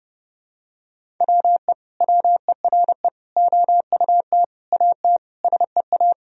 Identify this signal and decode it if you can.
Morse code, heard as a keyed tone, about 32 wpm, 700 Hz. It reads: WE WERE OUT AT SEA
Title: Sentences from Top 1000 Words - Review Part 1 - 30wpm